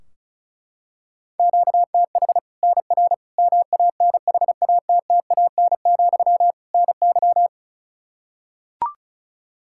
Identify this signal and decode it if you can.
Morse code, heard as a keyed tone, about 34 wpm, 700 Hz. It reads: QTH NR MANHATTAN, NY
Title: Practice QSO - Exchange 1 - Beginning - 35wpm